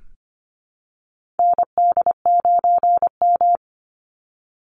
Morse code, heard as a keyed tone, about 25 wpm, 700 Hz. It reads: ND9M
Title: Call Signs Repeated 2x - 25wpm